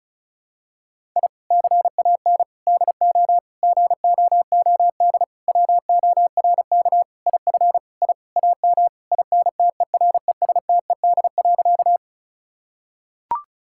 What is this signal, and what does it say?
Morse code, heard as a keyed tone, about 36 wpm, 700 Hz. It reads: I CAN DO GOOD WORK IF I AM INTERESTED.
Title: Sentences from Top 1000 Words - Review Part 6 - 35wpm